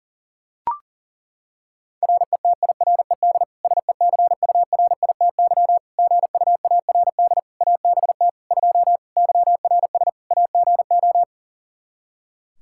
Morse code, heard as a keyed tone, about 37 wpm, 700 Hz. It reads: RETIRED SECURITY GUARD ABT 2 YRS AGO
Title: Practice QSO - Exchange 3 - More Info - 40wpm